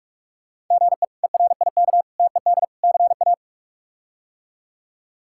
Morse code, heard as a keyed tone, about 46 wpm, 700 Hz. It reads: GE ERIK TED CA
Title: K1USN SST QSO Practice - 45wpm